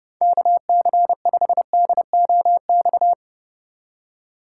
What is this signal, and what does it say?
Morse code, heard as a keyed tone, about 32 wpm, 700 Hz. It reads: KC5DOX
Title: USA Call Signs - 6 Characters - 2x Character Spacing - 30wpm